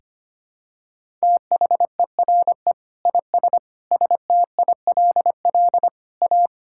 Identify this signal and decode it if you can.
Morse code, heard as a keyed tone, about 25 wpm, 700 Hz. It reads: THERE IS STILL A
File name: Sets of 9 Words - 25wpm